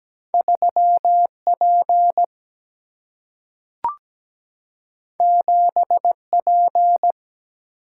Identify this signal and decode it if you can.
Morse code, heard as a keyed tone, about 17 wpm, 700 Hz. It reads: 3P 7P
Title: Single-Letter Single-Number 17wpm